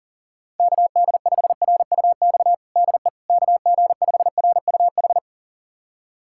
Morse code, heard as a keyed tone, about 40 wpm, 700 Hz. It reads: KD5RUX DE KG5RUH
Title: USA Callsign DE Callsign - 6 Characters - 40wpm